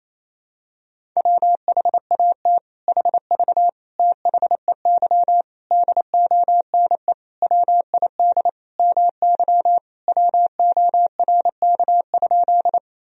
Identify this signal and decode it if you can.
Morse code, heard as a keyed tone, about 28 wpm, 700 Hz. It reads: WHAT HV THEY DONE WID MY WORK?
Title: Sentences from Top 500 Words and Abbreviations 28wpm